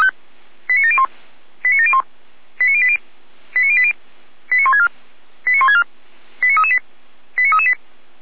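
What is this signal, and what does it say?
Five-tone selective call (ZVEI), heard as quick runs of five tones, tones of about 70 ms. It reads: 89701 89701 89080 89080 89164 89164 89338 89338
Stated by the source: Received by SDR; ZVEI five-tone code 89700.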